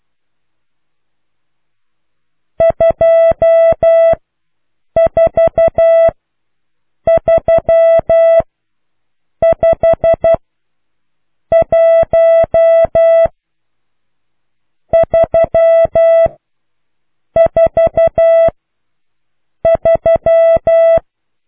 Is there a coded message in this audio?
Yes: Morse code from 2.6 to 21.0 s, heard as a keyed tone, about 12 wpm, 660 Hz. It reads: 2 4 3 5 1 3 4 3